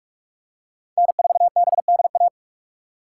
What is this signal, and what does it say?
Morse code, heard as a keyed tone, about 41 wpm, 700 Hz. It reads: N4BDA